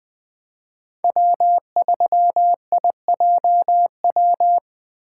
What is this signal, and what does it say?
Morse code, heard as a keyed tone, about 20 wpm, 700 Hz. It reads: W3IJW
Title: USA Call Signs - 5 Characters - 3x Character Spacing - 20wpm